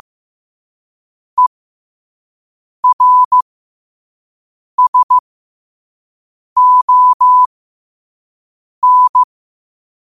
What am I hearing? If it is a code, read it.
Morse code, heard as a keyed tone, about 15 wpm, spaced as for 5 wpm, 1000 Hz. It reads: ERSON
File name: 5wpm